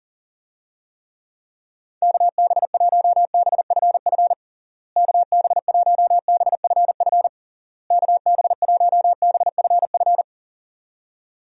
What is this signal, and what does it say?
Morse code, heard as a keyed tone, about 40 wpm, 700 Hz. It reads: KB1BFF KB1BFF KB1BFF
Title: USA Call Signs - 6 Characters - Repeated 3x - 40wpm